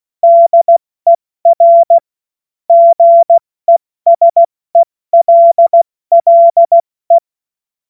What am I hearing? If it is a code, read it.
Morse code, heard as a keyed tone, about 16 wpm, 680 Hz. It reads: DER GESELLE
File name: Morsekurs / Witze / Witz 20. Januar 2026 70 F80